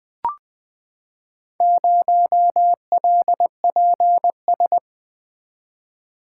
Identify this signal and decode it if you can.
Morse code, heard as a keyed tone, about 20 wpm, 700 Hz. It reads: 0LPS
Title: Four Letter-Number Combinations - 20wpm